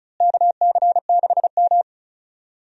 Morse code, heard as a keyed tone, about 36 wpm, 700 Hz. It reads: KC6M
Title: USA Call Signs - 4 Characters - Repeated 3x - 35wpm